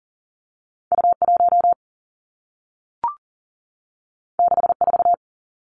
Morse code, heard as a keyed tone, about 39 wpm, 700 Hz. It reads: U1 64
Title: Two Challenging Letter-Number Combinations 40wpm